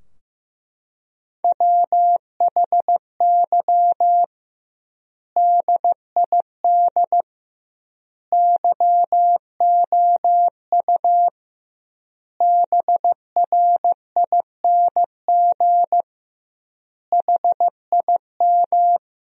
Morse code, heard as a keyed tone, about 15 wpm, 700 Hz. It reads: WHY DID YOU BRING HIM
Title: 2x Word-Spacing - Sentences from Top 500 Words - 15wpm